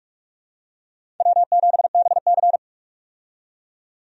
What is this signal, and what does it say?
Morse code, heard as a keyed tone, about 44 wpm, 700 Hz. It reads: W7BC
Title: USA Call Signs - 4 Characters - 45wpm